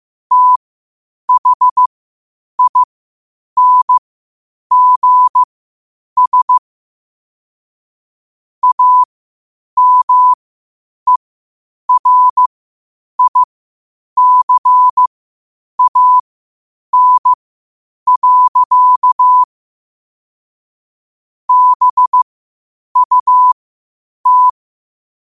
Morse code, heard as a keyed tone, about 15 wpm, spaced as for 8 wpm, 1000 Hz. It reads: THINGS AMERICAN. BUT
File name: qotdfiles / 8wpm